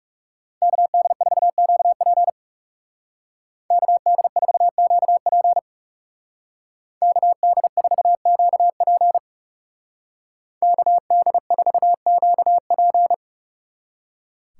Morse code, heard as a keyed tone, about 37 wpm, 700 Hz. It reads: KD4QP KD4QP KD4QP KD4QP
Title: Speed Racing - USA 2x2 Call Signs 30wpm